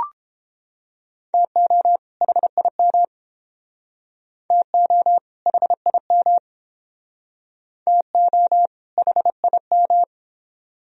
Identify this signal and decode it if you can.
Morse code, heard as a keyed tone, about 31 wpm, 700 Hz. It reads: TO HIM TO HIM TO HIM